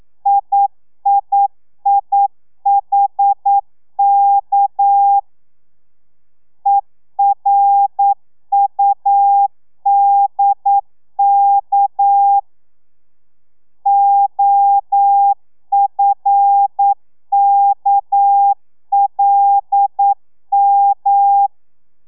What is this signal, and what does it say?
Morse code, heard as a keyed tone, about 9 wpm, 800 Hz. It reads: IIIHK ERUDK OFKLM
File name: telegrafia / i2tfj / Lez6 40